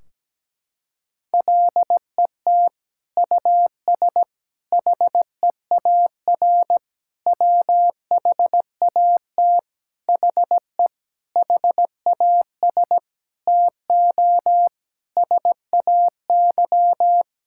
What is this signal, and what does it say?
Morse code, heard as a keyed tone, about 17 wpm, 700 Hz. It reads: LET US HEAR WHAT HE HAS TO SAY
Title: Sentences from Top 1600 Words 17wpm